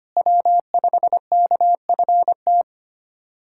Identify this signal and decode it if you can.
Morse code, heard as a keyed tone, about 25 wpm, 700 Hz. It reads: W5KFT
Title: K1USN SST QSO Practice - 25wpm